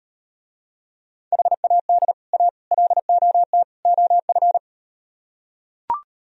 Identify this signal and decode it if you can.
Morse code, heard as a keyed tone, about 39 wpm, 700 Hz. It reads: HAD A LOT OF